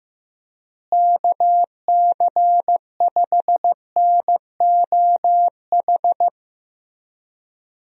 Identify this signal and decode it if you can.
Morse code, heard as a keyed tone, about 16 wpm, 700 Hz. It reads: KC5NOH
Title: USA Call Signs - 6 Characters - Repeated 3x - 15wpm